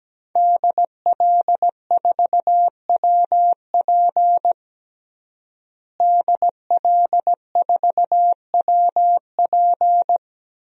Morse code, heard as a keyed tone, about 17 wpm, 700 Hz. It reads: DL4WP DL4WP
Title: DXCC-100 Call Signs - 5 Characters - Repeated 2x - 17wpm